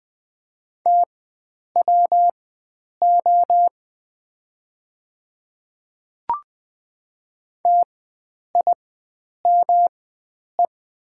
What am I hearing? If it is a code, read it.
Morse code, heard as a keyed tone, about 20 wpm, spaced as for 9 wpm, 700 Hz. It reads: TWO TIME
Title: 4x Character Spacing - Top 100 Words - 20wpm